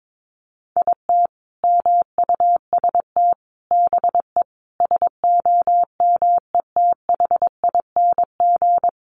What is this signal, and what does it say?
Morse code, heard as a keyed tone, about 22 wpm, 700 Hz. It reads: IT MUST BE SOMETHING
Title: Sentences from Top 600 Words 22wpm